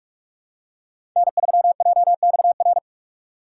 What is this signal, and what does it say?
Morse code, heard as a keyed tone, about 47 wpm, 700 Hz. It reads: N3JXR
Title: USA Call Signs - 2x Character Spacing - 45wpm